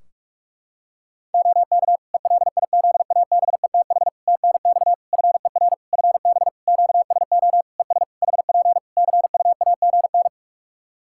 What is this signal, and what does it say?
Morse code, heard as a keyed tone, about 46 wpm, 700 Hz. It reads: OK ELIZABETH TNX FER FB QSO ES HP CUAGN
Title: Practice QSO - Exchange 4 - Ending - 45wpm